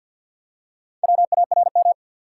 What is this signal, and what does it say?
Morse code, heard as a keyed tone, about 53 wpm, 700 Hz. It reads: WARM
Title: Four Letter Words - 50wpm